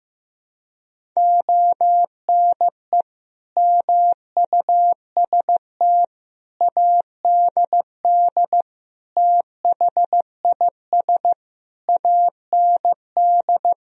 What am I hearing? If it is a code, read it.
Morse code, heard as a keyed tone, about 15 wpm, 700 Hz. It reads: ONE MUST ADD THIS AND
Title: Sentences from Top 1000 Words - Review Part 1 - 15wpm